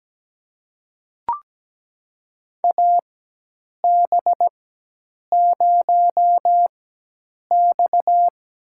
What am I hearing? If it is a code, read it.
Morse code, heard as a keyed tone, about 17 wpm, spaced as for 8 wpm, 700 Hz. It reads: AB0X
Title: USA Call Signs - 4x Character Spacing - 17wpm